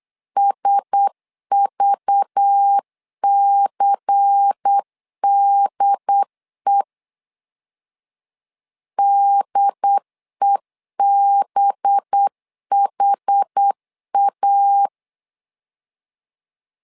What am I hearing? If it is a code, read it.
Morse code, heard as a keyed tone, about 8 wpm, 800 Hz. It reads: SVCDE DEBHA
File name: LEZIONI / 5 LBVC / 5 LBVC 8wpm 2935